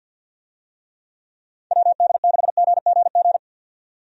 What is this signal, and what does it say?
Morse code, heard as a keyed tone, about 51 wpm, 700 Hz. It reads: WD6GGG
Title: USA Call Signs - 6 Characters - 50wpm